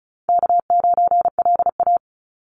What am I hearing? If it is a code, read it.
Morse code, heard as a keyed tone, about 36 wpm, 700 Hz. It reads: K9LA
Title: Call Signs - 35wpm